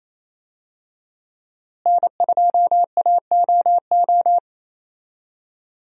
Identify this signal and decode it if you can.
Morse code, heard as a keyed tone, about 28 wpm, 700 Hz. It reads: N2AOO
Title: USA Call Signs - 5 Characters - Repeated 2x - 28wpm